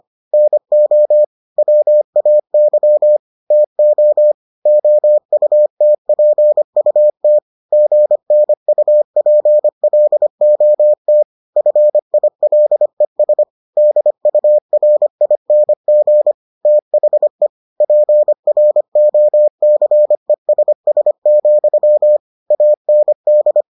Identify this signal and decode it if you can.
Morse code, heard as a keyed tone, about 25 wpm, 600 Hz. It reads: NO WAY TO OUTPUT GNUPLOT FILES DURING THE PROCESS, AND